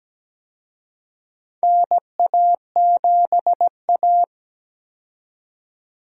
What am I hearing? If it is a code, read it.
Morse code, heard as a keyed tone, about 17 wpm, 700 Hz. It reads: NA7A
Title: USA Call Signs - 4 Characters - 17wpm